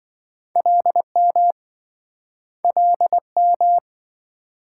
Morse code, heard as a keyed tone, about 27 wpm, 700 Hz. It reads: LM LM